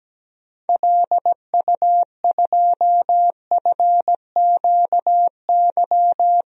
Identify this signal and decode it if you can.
Morse code, heard as a keyed tone, about 17 wpm, 700 Hz. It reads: LU2FQY
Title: DXCC-100 Call Signs - 6 Characters - Repeated 2x - 17wpm